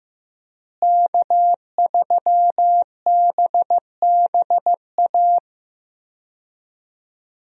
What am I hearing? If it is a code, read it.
Morse code, heard as a keyed tone, about 15 wpm, 700 Hz. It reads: K3BBA
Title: USA Call Signs - 5 Characters - Repeated 3x - 15wpm